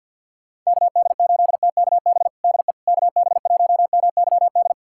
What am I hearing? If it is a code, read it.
Morse code, heard as a keyed tone, about 51 wpm, 700 Hz. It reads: KD8TKB DE KB1MYD
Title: USA Callsign DE Callsign - 6 Characters - 50wpm